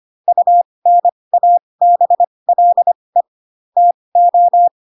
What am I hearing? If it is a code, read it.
Morse code, heard as a keyed tone, about 25 wpm, spaced as for 20 wpm, 700 Hz. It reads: UNABLE TO
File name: Pinocchio / Pinocchio ch8 15 25 ik1hge